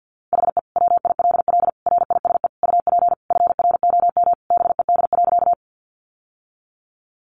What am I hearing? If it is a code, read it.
Morse code, heard as a keyed tone, about 50 wpm, 700 Hz. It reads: HE WILL RISE UP FROM BED.
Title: Sentences from Top 900 Words 50wpm